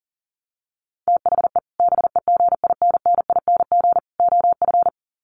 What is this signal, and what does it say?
Morse code, heard as a keyed tone, about 40 wpm, 700 Hz. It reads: THE BEGINNING OF
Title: Sets of 3 Words 40wpm